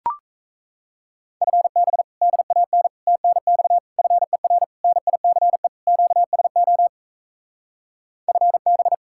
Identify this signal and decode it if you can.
Morse code, heard as a keyed tone, about 38 wpm, 700 Hz. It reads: FB DAN TNX FER NICE QSO FB